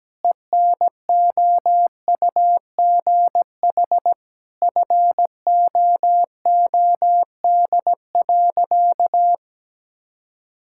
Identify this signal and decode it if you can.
Morse code, heard as a keyed tone, about 17 wpm, 700 Hz. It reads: ENOUGH FOOD.
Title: Sentences from Top 1000 Words - Review Part 5 - 17wpm